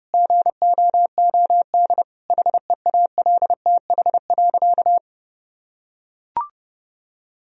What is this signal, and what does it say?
Morse code, heard as a keyed tone, about 30 wpm, 700 Hz. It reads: GOOD HEALTH.
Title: Sentences from Top 1000 Words - Review Part 3 - 30wpm